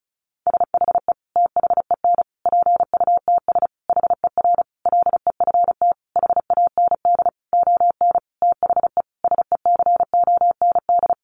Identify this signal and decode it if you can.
Morse code, heard as a keyed tone, about 34 wpm, 700 Hz. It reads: SHE THEN PUTS HER LEFT HAND ON THE SECOND
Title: Easy Phrases 35wpm